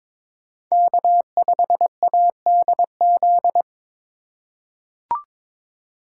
Morse code, heard as a keyed tone, about 21 wpm, 700 Hz. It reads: K5ADZ
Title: USA Call Signs - 2x Character Spacing - 22wpm